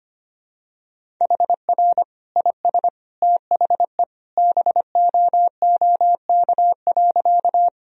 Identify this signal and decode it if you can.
Morse code, heard as a keyed tone, about 25 wpm, 700 Hz. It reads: HR IS THE BOOK.
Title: Sentences from Top 300 Words and Abbreviations 25wpm